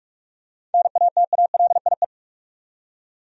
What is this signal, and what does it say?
Morse code, heard as a keyed tone, about 46 wpm, 700 Hz. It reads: NATALIE
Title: USA Common Names - 45wpm